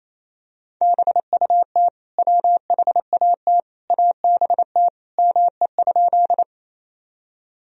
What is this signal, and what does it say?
Morse code, heard as a keyed tone, about 28 wpm, 700 Hz. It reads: BUT WHAT ABT ME?
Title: Sentences from Top 200 Words and Abbreviations 28wpm